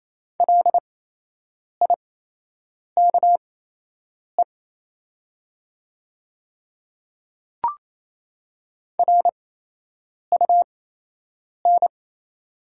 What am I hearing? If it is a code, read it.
Morse code, heard as a keyed tone, about 28 wpm, spaced as for 8 wpm, 700 Hz. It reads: LIKE RUN
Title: 8x Character Spacing - Top 300 Words - 28wpm